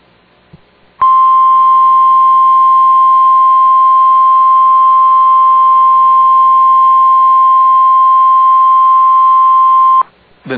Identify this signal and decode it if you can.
Emergency Alert System (SAME) tones (the 1050 Hz attention tone), with speech. It is heard from 1.0 to 10.0 s.